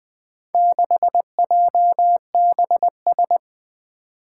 Morse code, heard as a keyed tone, about 20 wpm, 700 Hz. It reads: 6JBS
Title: Four Challenging Letter-Number Combinations 20wpm